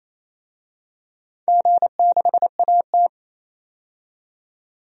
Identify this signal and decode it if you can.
Morse code, heard as a keyed tone, about 28 wpm, 700 Hz. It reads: G6AT
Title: DXCC-100 Call Signs - 4 Characters - 28wpm